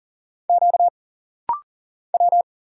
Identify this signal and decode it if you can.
Morse code, heard as a keyed tone, about 40 wpm, 700 Hz. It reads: Q W
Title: Single Letters - Rapid-Fire 40wpm